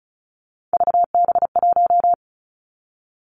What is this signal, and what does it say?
Morse code, heard as a keyed tone, about 36 wpm, 700 Hz. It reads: VB1
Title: Three Challenging Letter-Number Combinations 35wpm